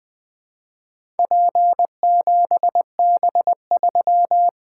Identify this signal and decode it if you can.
Morse code, heard as a keyed tone, about 20 wpm, 700 Hz. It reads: P7B3
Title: Four Letter-Number Combinations - 20wpm